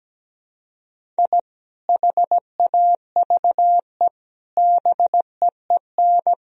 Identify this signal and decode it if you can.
Morse code, heard as a keyed tone, about 17 wpm, 700 Hz. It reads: I HAVE BEEN
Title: Sentences from Top 1000 Words - Review Part 3 - 17wpm